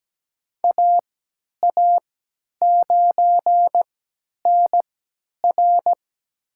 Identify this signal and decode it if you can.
Morse code, heard as a keyed tone, about 17 wpm, spaced as for 10 wpm, 700 Hz. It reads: AA9NR
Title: USA Call Signs - 5 Characters - 3x Character Spacing - 17wpm